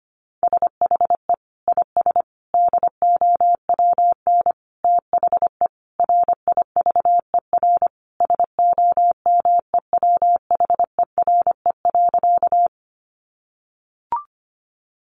Morse code, heard as a keyed tone, about 25 wpm, 700 Hz. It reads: SHE IS DOWN THE RIVER SOMEWHERE.
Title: Sentences from Top 1400 Words 25wpm